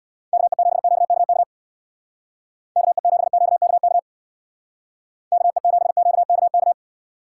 Morse code, heard as a keyed tone, about 59 wpm, 700 Hz. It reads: KE7QKK KE7QKK KE7QKK
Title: Speed Racing - USA 2x3 Call Signs 50wpm